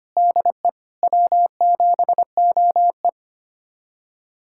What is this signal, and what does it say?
Morse code, heard as a keyed tone, about 25 wpm, 700 Hz. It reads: DE W7OE
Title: USA Callsign DE Callsign - 4 Characters - 25wpm